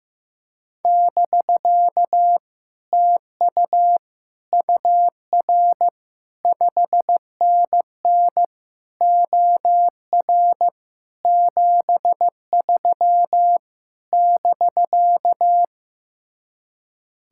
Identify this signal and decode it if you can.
Morse code, heard as a keyed tone, about 15 wpm, 700 Hz. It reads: <BK> TU UR 5NN OR 73 <BK>
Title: Speed-Racing Common QSO Phrases Encore - 15wpm